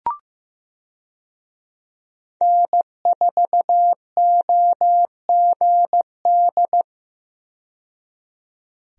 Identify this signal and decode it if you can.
Morse code, heard as a keyed tone, about 15 wpm, 700 Hz. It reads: N4OGD
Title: USA Call Signs - 5 Characters - 15wpm